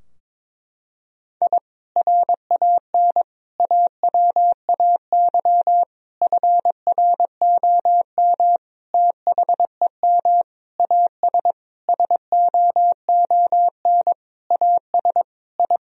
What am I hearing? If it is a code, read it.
Morse code, heard as a keyed tone, about 21 wpm, 700 Hz. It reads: I RAN AWAY FROM THEM AS SOON AS I
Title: Sentences from Top 1000 Words - Review Part 3 - 22wpm